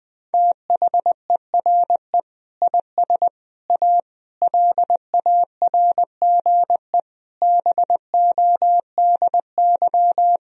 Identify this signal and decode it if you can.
Morse code, heard as a keyed tone, about 21 wpm, 700 Hz. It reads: THERE IS A LARGE BODY
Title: Sentences from Top 1000 Words - Review Part 4 - 20wpm